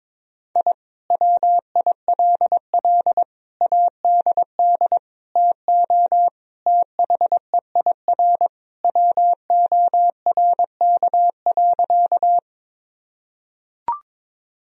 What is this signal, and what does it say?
Morse code, heard as a keyed tone, about 21 wpm, 700 Hz. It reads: I WILL ADD TO THEIR WORK.
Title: Sentences from Top 1000 Words - Review Part 1 - 22wpm